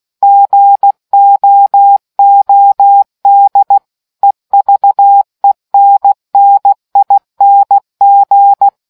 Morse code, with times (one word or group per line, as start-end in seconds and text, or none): 0.2-3.8 GOOD
4.2-8.7 EVENNING